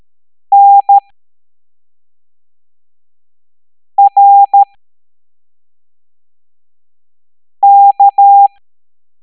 Morse code, heard as a keyed tone, about 13 wpm, 800 Hz. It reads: N R K
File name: hamdata / L01-EX02-ETANTK